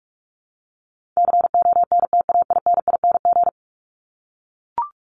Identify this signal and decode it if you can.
Morse code, heard as a keyed tone, about 44 wpm, 700 Hz. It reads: CONTAINING
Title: ING Suffix - Top 100 Words with Base Words 45wpm